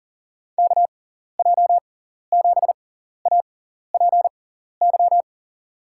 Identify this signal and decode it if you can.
Morse code, heard as a keyed tone, about 38 wpm, spaced as for 14 wpm, 700 Hz. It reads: KJ7APY
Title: USA Call Signs - 6x Character Spacing - 40wpm